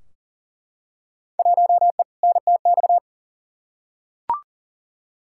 Morse code, heard as a keyed tone, about 39 wpm, 700 Hz. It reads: 1E NTX
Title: Field Day Exchanges - 40wpm